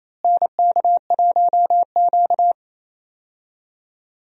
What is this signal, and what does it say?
Morse code, heard as a keyed tone, about 27 wpm, 700 Hz. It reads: NK1Q
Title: USA Call Signs - 4 Characters - Repeated 3x - 28wpm